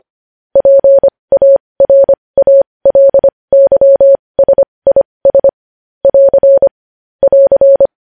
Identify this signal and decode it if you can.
Morse code, heard as a keyed tone, about 25 wpm, spaced as for 20 wpm, 550 Hz. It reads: PARALYSIS <AR> <AR>
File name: podcast / News headlines-1x-25x20